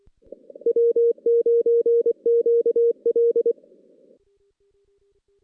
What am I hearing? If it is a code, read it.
Morse code, heard as a keyed tone, about 23 wpm, 420 Hz. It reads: W9QL
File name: 2016 JAN CW NAQP / 2016 Jan naqp k5ot - 0-6